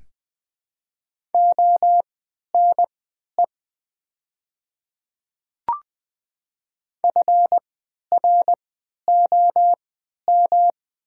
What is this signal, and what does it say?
Morse code, heard as a keyed tone, about 20 wpm, spaced as for 11 wpm, 700 Hz. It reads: ONE FROM
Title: 3x Character Spacing - Top 100 Words - 20wpm